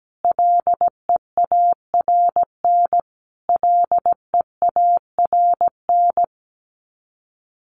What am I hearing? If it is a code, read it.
Morse code, heard as a keyed tone, about 17 wpm, 700 Hz. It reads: LEARN LEARN
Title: Top 300 Words Repeated 17wpm